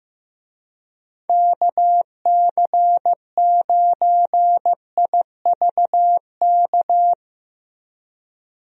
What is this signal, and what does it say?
Morse code, heard as a keyed tone, about 15 wpm, 700 Hz. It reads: KC9IVK
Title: USA Call Signs - 6 Characters - Repeated 3x - 15wpm